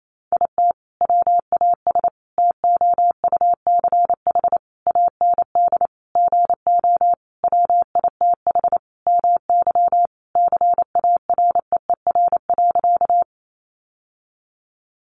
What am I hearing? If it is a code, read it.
Morse code, heard as a keyed tone, about 28 wpm, 700 Hz. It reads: IT WAS TOUCH AND GO WITH MY CAREER.
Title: Sentences from Top 1200 Words 28wpm